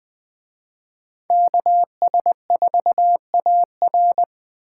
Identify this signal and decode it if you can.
Morse code, heard as a keyed tone, about 20 wpm, 700 Hz. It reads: KS4AR
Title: Speed Racing - USA 2x2 Call Signs 17wpm